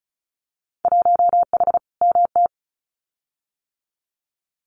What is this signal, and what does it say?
Morse code, heard as a keyed tone, about 36 wpm, 700 Hz. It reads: 1H MT
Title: Winter Field Day Exchanges Encore 35wpm